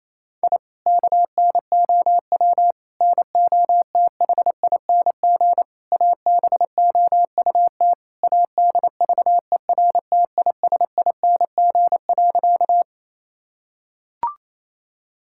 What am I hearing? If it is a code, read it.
Morse code, heard as a keyed tone, about 28 wpm, 700 Hz. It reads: I KNOW NOTHING ABOUT ADVERTISING.
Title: Sentences from Top 1000 Words - Review Part 6 - 28wpm